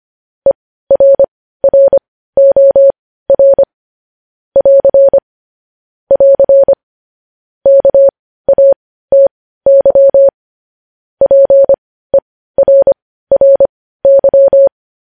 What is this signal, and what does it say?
Morse code, heard as a keyed tone, about 25 wpm, spaced as for 15 wpm, 550 Hz. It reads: ERROR <AR> <AR> KATY PERRY
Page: news headlines in morse code at 15 wpm